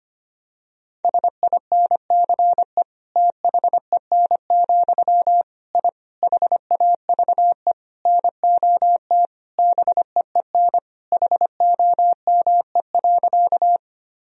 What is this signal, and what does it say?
Morse code, heard as a keyed tone, about 25 wpm, 700 Hz. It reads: SINCE THEN, I HAVE NOT BEEN HOME.
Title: Sentences from Top 1000 Words - Review Part 1 - 25wpm